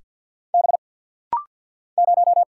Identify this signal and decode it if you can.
Morse code, heard as a keyed tone, about 50 wpm, 700 Hz. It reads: B 0
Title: Single Letter-Number - Rapid-Fire 50wpm